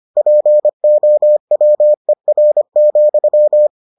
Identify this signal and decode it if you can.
Morse code, heard as a keyed tone, about 25 wpm, 600 Hz. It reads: POWER,